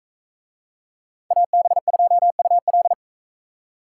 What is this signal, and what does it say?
Morse code, heard as a keyed tone, about 42 wpm, 700 Hz. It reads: AB2UL